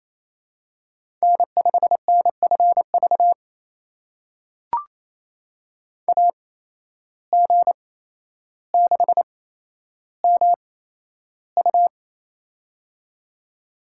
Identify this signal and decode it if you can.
Morse code, heard as a keyed tone, about 28 wpm, spaced as for 12 wpm, 700 Hz. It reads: N5NFV AG6MU
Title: USA Call Signs - 5 Characters - 8x Character Spacing - 28wpm